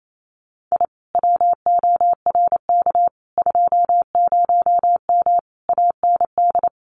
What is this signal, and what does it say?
Morse code, heard as a keyed tone, about 28 wpm, 700 Hz. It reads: I WORK 20M AND
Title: Common QSO Phrases Encore 28wpm